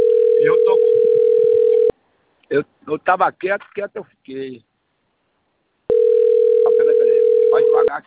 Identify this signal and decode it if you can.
Ringback tone (North America), 440 + 480 Hz, with speech.
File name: out- - - - - .13688